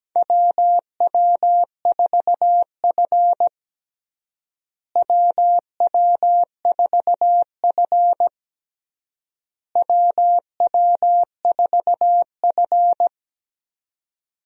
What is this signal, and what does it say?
Morse code, heard as a keyed tone, about 17 wpm, 700 Hz. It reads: WW4F WW4F WW4F
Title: USA Call Signs - 4 Characters - Repeated 3x - 17wpm